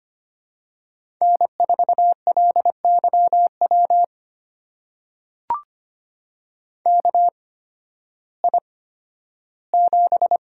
Morse code, heard as a keyed tone, about 25 wpm, spaced as for 13 wpm, 700 Hz. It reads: N4LYW KI7
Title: USA Call Signs - 8x Character Spacing - 25wpm